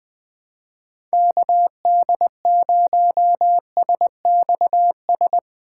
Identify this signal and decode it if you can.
Morse code, heard as a keyed tone, about 21 wpm, 700 Hz. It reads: KD0SXS
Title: USA Call Signs - 6 Characters - 3x Character Spacing - 20wpm